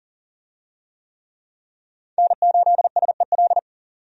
Morse code, heard as a keyed tone, about 40 wpm, 700 Hz. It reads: N8SEL
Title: USA Call Signs - 5 Characters - 40wpm